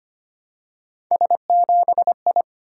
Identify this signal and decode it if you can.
Morse code, heard as a keyed tone, about 25 wpm, 700 Hz. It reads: S7I
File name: Three Letter-Number Combinations - 25wpm